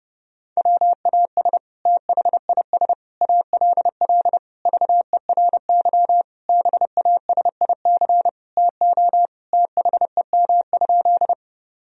Morse code, heard as a keyed tone, about 30 wpm, 700 Hz. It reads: WAS THIS ALL VERY BASIC TO THEM?
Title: Sentences from Top 1000 Words - Review Part 3 - 30wpm